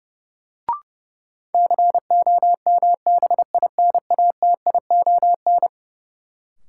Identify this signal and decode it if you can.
Morse code, heard as a keyed tone, about 30 wpm, 700 Hz. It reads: COMBINATION
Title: Sentences from Top 1000 Words - Review Part 6 - New Word Drill - 30wpm